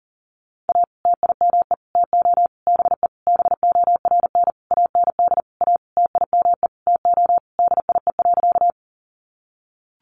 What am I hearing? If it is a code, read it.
Morse code, heard as a keyed tone, about 40 wpm, 700 Hz. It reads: A TIME TO BE BORN AND A TIME TO DIE.
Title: Sentences from Top 1300 Words 40wpm